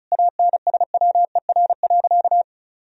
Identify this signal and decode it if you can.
Morse code, heard as a keyed tone, about 34 wpm, 700 Hz. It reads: ANSWER.
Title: Sentences from Top 1000 Words - Review Part 4 - 35wpm